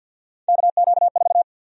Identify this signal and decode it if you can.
Morse code, heard as a keyed tone, about 53 wpm, 700 Hz. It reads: KQ4
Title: USA Call Sign Prefixes - Repeated 2x - 50wpm